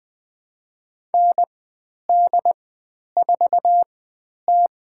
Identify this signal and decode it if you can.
Morse code, heard as a keyed tone, about 20 wpm, spaced as for 10 wpm, 700 Hz. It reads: ND4T
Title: 102 Lesson 17g - Letter u - Call Signs 20wpm